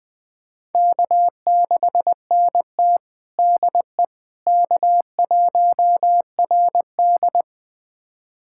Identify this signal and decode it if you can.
Morse code, heard as a keyed tone, about 21 wpm, 700 Hz. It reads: K6NT DE K1RD
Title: USA Callsign DE Callsign - 4 Characters - 20wpm